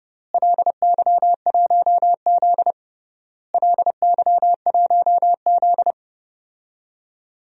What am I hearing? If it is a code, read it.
Morse code, heard as a keyed tone, about 30 wpm, 700 Hz. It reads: LY1Z LY1Z
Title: DXCC-100 Call Signs - 4 Characters - Repeated 3x - 30wpm